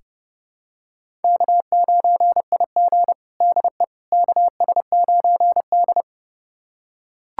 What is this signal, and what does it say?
Morse code, heard as a keyed tone, about 29 wpm, 700 Hz. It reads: K9IG DE KS9D
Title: USA Callsign DE Callsign - 4 Characters - 30wpm